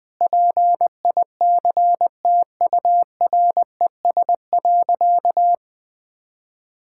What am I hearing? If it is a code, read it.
Morse code, heard as a keyed tone, about 21 wpm, 700 Hz. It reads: PICTURES.
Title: Sentences with 7-Word Phrases - 20wpm